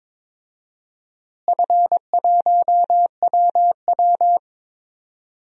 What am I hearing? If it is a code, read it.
Morse code, heard as a keyed tone, about 22 wpm, 700 Hz. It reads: F1WW
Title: DXCC-100 Call Signs - 4 Characters - Repeated 3x - 22wpm